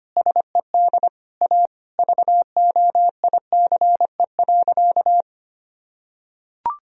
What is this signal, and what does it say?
Morse code, heard as a keyed tone, about 25 wpm, 700 Hz. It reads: SED A VOICE.
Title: Sentences from Top 500 Words and Abbreviations 25wpm